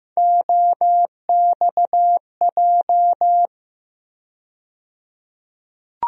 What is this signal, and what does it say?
Morse code, heard as a keyed tone, about 15 wpm, 700 Hz. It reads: OXJ
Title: USA Call Sign Postfixes - 3 Letters 15wpm